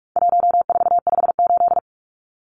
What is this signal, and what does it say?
Morse code, heard as a keyed tone, about 44 wpm, 700 Hz. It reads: 1458
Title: Numbers - 4-Digit 45wpm